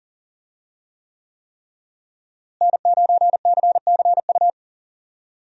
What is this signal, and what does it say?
Morse code, heard as a keyed tone, about 40 wpm, 700 Hz. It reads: N9CCU
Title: USA Call Signs - 5 Characters - Repeated 3x - 40wpm